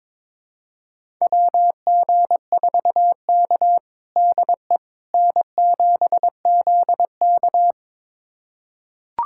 Morse code, heard as a keyed tone, about 22 wpm, 700 Hz. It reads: WG4K DE N7ZK
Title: USA Callsign DE Callsign - 4 Characters - 22wpm